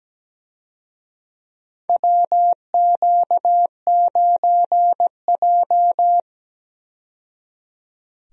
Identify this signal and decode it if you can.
Morse code, heard as a keyed tone, about 17 wpm, 700 Hz. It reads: WQ9J
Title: USA Call Signs - 4 Characters - 17wpm